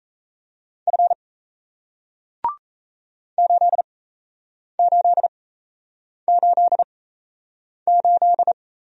Morse code, heard as a keyed tone, about 37 wpm, spaced as for 8 wpm, 700 Hz. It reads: F 8888